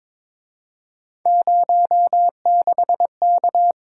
Morse code, heard as a keyed tone, about 26 wpm, 700 Hz. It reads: 06K
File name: Speed-Racing - Three Letter-Number Combinations - 22wpm